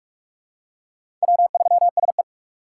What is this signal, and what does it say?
Morse code, heard as a keyed tone, about 44 wpm, 700 Hz. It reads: W3SE
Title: USA Call Signs - 4 Characters - 3x Character Spacing - 45wpm